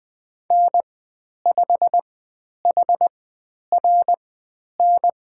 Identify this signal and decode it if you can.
Morse code, heard as a keyed tone, about 20 wpm, spaced as for 10 wpm, 700 Hz. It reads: N5HRN
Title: 102 Lesson 17g - Letter u - Call Signs 20wpm